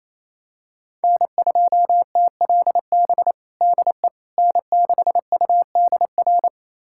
Morse code, heard as a keyed tone, about 28 wpm, 690 Hz. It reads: N2TLB DE N6UDR
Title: USA Callsign DE Callsign - 5 Characters - 28wpm